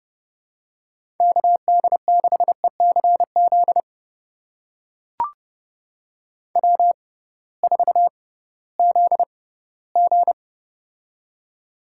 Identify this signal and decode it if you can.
Morse code, heard as a keyed tone, about 32 wpm, spaced as for 18 wpm, 700 Hz. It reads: KD6ECZ W4ZG
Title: USA Call Signs - 6x Character Spacing - 30wpm